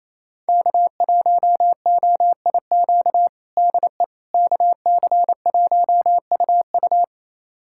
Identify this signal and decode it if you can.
Morse code, heard as a keyed tone, about 28 wpm, 700 Hz. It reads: K1OIQ DE KC1UU
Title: USA Callsign DE Callsign - 5 Characters - 28wpm